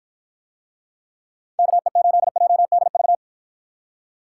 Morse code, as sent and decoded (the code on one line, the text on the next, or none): -.- . ---.. .--- -.. ...-
KE8JDV